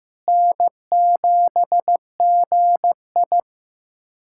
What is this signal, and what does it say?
Morse code, heard as a keyed tone, about 15 wpm, 700 Hz. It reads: N7GI
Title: USA Callsign DE Callsign - 4 Characters - 15wpm